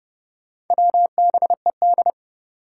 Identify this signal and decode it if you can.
Morse code, heard as a keyed tone, about 29 wpm, 700 Hz. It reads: WBED
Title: Four Challenging Letter-Number Combinations 30wpm